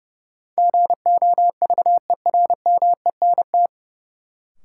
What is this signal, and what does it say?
Morse code, heard as a keyed tone, about 30 wpm, 700 Hz. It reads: GOVERMENT
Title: Sentences from Top 1300 Words - New Word Drill - 30wpm